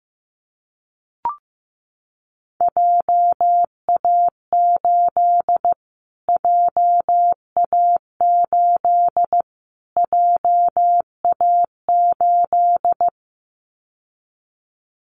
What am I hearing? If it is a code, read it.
Morse code, heard as a keyed tone, about 15 wpm, 700 Hz. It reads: JA8 JA8 JA8
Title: Callsign Prefixes for Top 25 DXCC - Repeated 3x - 15wpm